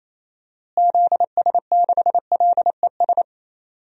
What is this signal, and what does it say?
Morse code, heard as a keyed tone, about 28 wpm, 700 Hz. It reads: ZS6LES
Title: DXCC-100 Call Signs - 6 Characters - Repeated 3x - 28wpm